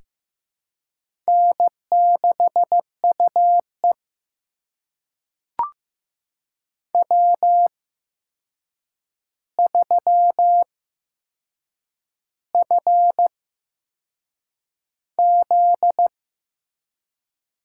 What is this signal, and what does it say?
Morse code, heard as a keyed tone, about 15 wpm, spaced as for 6 wpm, 700 Hz. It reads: N6UE W3FZ
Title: USA Call Signs - 4 Characters - 8x Character Spacing - 15wpm